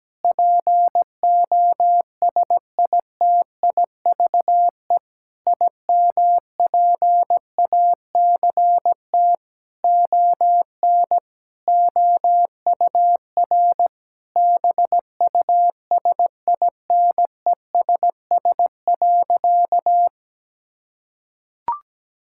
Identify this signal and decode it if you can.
Morse code, heard as a keyed tone, about 17 wpm, 700 Hz. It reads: POSITIVE IMPACT ON OUR BUSINESS.
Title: Sentences from Top 1000 Words - Review Part 6 - 17wpm